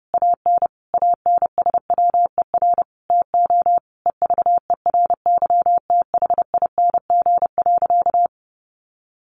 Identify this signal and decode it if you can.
Morse code, heard as a keyed tone, about 30 wpm, 700 Hz. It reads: AN ANSWER TO EVERYTHING.
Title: Sentences from Top 700 Words 30wpm